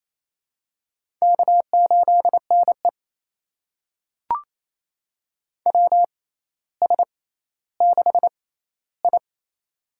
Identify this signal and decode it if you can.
Morse code, heard as a keyed tone, about 28 wpm, spaced as for 14 wpm, 700 Hz. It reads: K8NE WS6I
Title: USA Call Signs - 4 Characters - 6x Character Spacing - 28wpm